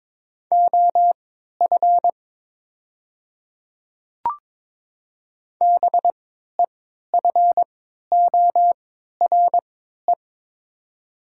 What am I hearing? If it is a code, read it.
Morse code, heard as a keyed tone, about 23 wpm, spaced as for 13 wpm, 700 Hz. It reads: OF BEFORE
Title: 3x Character Spacing - Top 200 Words - 22wpm